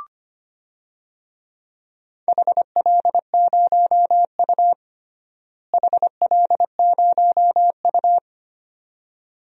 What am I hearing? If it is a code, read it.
Morse code, heard as a keyed tone, about 25 wpm, 700 Hz. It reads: HL0U HL0U
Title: DXCC-100 Call Signs - 4 Characters - Repeated 2x - 25wpm